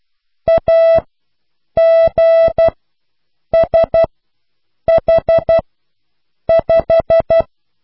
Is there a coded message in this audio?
Yes: Morse code from 0.5 to 7.4 s, heard as a keyed tone, about 12 wpm, spaced as for 7 wpm, 660 Hz. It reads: AGSH5